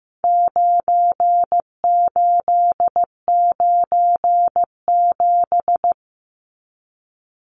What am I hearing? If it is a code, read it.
Morse code, heard as a keyed tone, about 15 wpm, 700 Hz. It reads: 9897
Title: Numbers - 4-Digit 15wpm